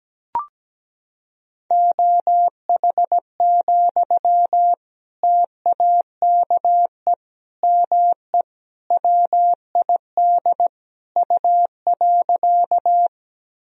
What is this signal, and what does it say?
Morse code, heard as a keyed tone, about 17 wpm, 700 Hz. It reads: OH, TAKE ME WID U.
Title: Sentences from Top 500 Words and Abbreviations 17wpm